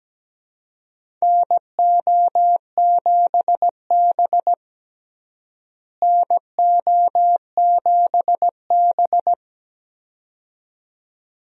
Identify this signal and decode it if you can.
Morse code, heard as a keyed tone, about 17 wpm, 700 Hz. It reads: NO7B NO7B
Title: USA Call Signs - 4 Characters - Repeated 2x - 17wpm